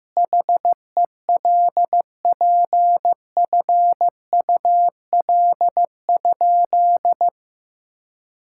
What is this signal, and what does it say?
Morse code, heard as a keyed tone, about 15 wpm, 700 Hz. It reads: HELPFUL?
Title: Sentences from Top 1000 Words - Review Part 6 - 15wpm